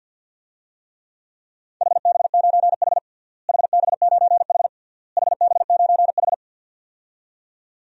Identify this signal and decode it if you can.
Morse code, heard as a keyed tone, about 52 wpm, 700 Hz. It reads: HB9H HB9H HB9H
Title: DXCC-100 Call Signs - 4 Characters - Repeated 3x - 50wpm